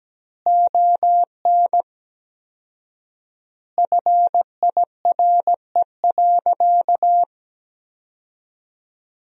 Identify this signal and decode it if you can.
Morse code, heard as a keyed tone, about 17 wpm, 700 Hz. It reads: ON FIRE.
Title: 4x Word-Spacing - Sentences from Top 400 Words - 17wpm